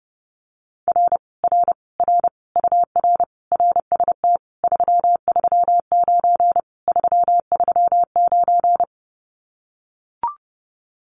Morse code, heard as a keyed tone, about 30 wpm, 700 Hz. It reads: R R R UR RST 339 339
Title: Common QSO Phrases Encore 30wpm